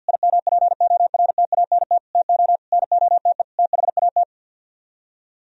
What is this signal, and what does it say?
Morse code, heard as a keyed tone, about 49 wpm, 700 Hz. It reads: IMPORTANT TO NOTE THAT
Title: Speed-Racing - Sets of 5 Words 50wpm